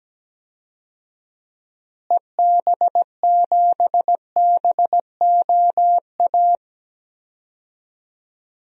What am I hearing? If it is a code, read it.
Morse code, heard as a keyed tone, about 17 wpm, 700 Hz. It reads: EB7BOA
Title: DXCC-100 Call Signs - 6 Characters - Repeated 3x - 17wpm